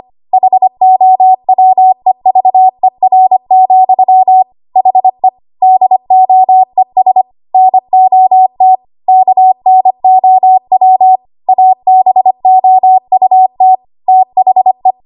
Morse code, heard as a keyed tone, about 25 wpm, 750 Hz. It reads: HOWEVER, HE DOES NOT KNOW ABOUT THE
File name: Archive / 25 WPM / 25WPM